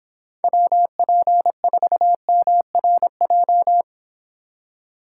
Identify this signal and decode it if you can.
Morse code, heard as a keyed tone, about 25 wpm, 700 Hz. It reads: WP4MRJ